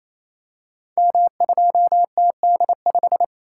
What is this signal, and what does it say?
Morse code, heard as a keyed tone, about 27 wpm, 700 Hz. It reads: M2TD5
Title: Code Groups - 5 Letter:Numbers - 28wpm